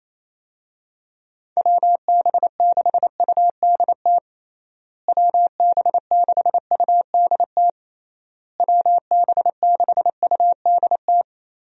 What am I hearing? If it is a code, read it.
Morse code, heard as a keyed tone, about 28 wpm, 700 Hz. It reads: WB6UDT WB6UDT WB6UDT
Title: USA Call Signs - 6 Characters - Repeated 3x - 28wpm